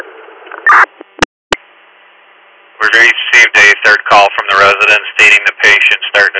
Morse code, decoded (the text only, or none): none